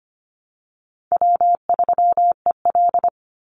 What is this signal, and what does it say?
Morse code, heard as a keyed tone, about 25 wpm, 700 Hz. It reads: W3EL
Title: Easy Call Signs Repeated 2x - 25wpm